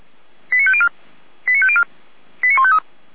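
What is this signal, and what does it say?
Five-tone selective call (ZVEI), heard as quick runs of five tones, tones of about 70 ms. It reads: 89504 89504 89153
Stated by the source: received on SDR, ZVEI five-tone selective call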